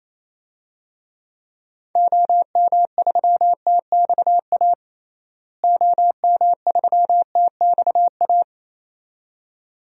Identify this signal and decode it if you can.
Morse code, heard as a keyed tone, about 28 wpm, 700 Hz. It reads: OM3TXA OM3TXA
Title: DXCC-100 Call Signs - 6 Characters - Repeated 2x - 28wpm